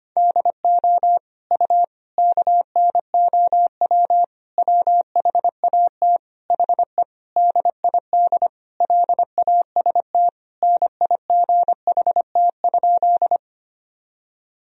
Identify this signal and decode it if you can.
Morse code, heard as a keyed tone, about 25 wpm, 700 Hz. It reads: DO U KNOW WHAT HE DID LAST NIGHT?
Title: Sentences from Top 300 Words and Abbreviations 25wpm